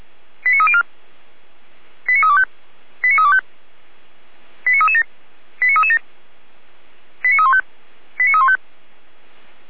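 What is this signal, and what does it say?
Five-tone selective call (ZVEI), heard as quick runs of five tones, tones of about 70 ms. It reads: 89304 89325 89325 89337 89337 89315 89315
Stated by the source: ZVEI five-tone selective call 89320, received by SDR.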